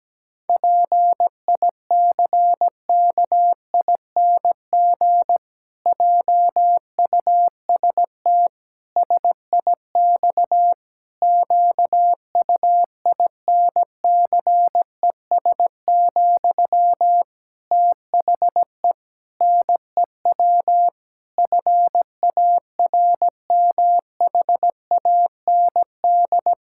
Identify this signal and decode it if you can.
Morse code, heard as a keyed tone, about 17 wpm, 700 Hz. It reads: PICKING JUST SIX QUINCES, THE NEW FARMHAND
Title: Pangram Challenge v2 - 17wpm